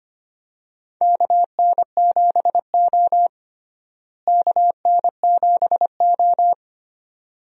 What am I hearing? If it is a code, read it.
Morse code, heard as a keyed tone, about 25 wpm, 700 Hz. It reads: KN7O KN7O
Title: USA Call Signs - 4 Characters - Repeated 2x - 25wpm